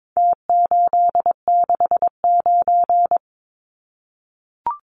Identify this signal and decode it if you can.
Morse code, heard as a keyed tone, about 22 wpm, 700 Hz. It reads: T869
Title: Numbers - Contest Style - 4-Digit 22wpm